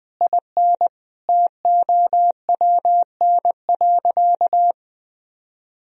Morse code, heard as a keyed tone, about 20 wpm, 700 Hz. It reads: IN TOWN.
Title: Sentences from Top 1000 Words - Review Part 4 - 20wpm